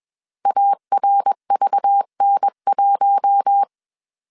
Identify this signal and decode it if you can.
Morse code, heard as a keyed tone, about 21 wpm, 800 Hz. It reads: ARVN1
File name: CW / DET / 1 6 1500C S B 20wpm 2048